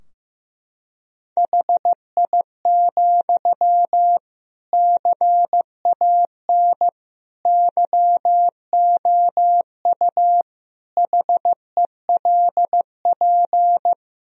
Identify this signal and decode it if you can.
Morse code, heard as a keyed tone, about 15 wpm, 700 Hz. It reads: HI, CAN YOU HELP
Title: Sentences from Top 1000 Words - Review Part 3 - 15wpm